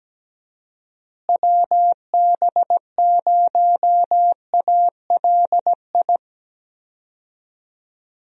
Morse code, heard as a keyed tone, about 17 wpm, 700 Hz. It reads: WB0ALI
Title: USA Call Signs - 6 Characters - Repeated 2x - 17wpm